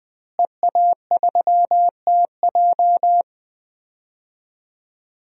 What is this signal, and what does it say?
Morse code, heard as a keyed tone, about 21 wpm, 700 Hz. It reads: EA3TJ
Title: DXCC-100 Call Signs - 5 Characters - Repeated 3x - 20wpm